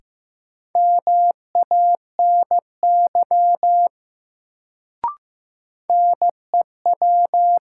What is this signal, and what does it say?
Morse code, heard as a keyed tone, about 15 wpm, 700 Hz. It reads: MANY NEW
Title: Sets of 2 Words - New Word Drill - 15wpm